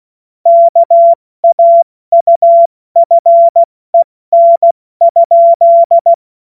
Morse code, heard as a keyed tone, about 16 wpm, 680 Hz. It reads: KAUFEN?
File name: Morsekurs / Witze / Witz 03. August 2021, 70 F80